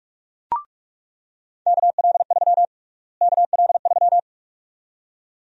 Morse code, heard as a keyed tone, about 42 wpm, 700 Hz. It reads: KL3 KL3
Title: USA Call Sign Prefixes - Repeated 2x - 45wpm